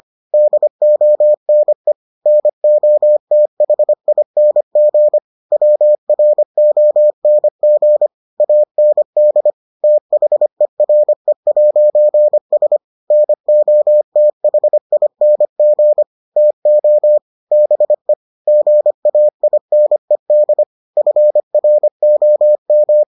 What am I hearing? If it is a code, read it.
Morse code, heard as a keyed tone, about 25 wpm, 600 Hz. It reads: DONE NOTHING WRONG AND THERE'S NOTHING TO BE GAINED FROM